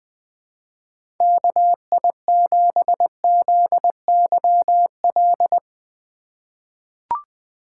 Morse code, heard as a keyed tone, about 20 wpm, 700 Hz. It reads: KI7ZYL
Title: USA Call Signs - 3x Character Spacing - 20wpm